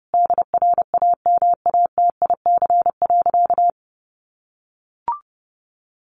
Morse code, heard as a keyed tone, about 30 wpm, 700 Hz. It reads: DRAMATIC.
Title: Sentences from Top 1300 Words 30wpm